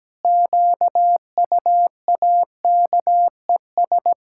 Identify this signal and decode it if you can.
Morse code, heard as a keyed tone, about 17 wpm, 700 Hz. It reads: QUAKES
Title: Pangram Challenge v2 - 17wpm